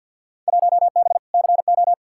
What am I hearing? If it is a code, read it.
Morse code, heard as a keyed tone, about 53 wpm, 700 Hz. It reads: 1B CO
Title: Field Day Exchanges - 50wpm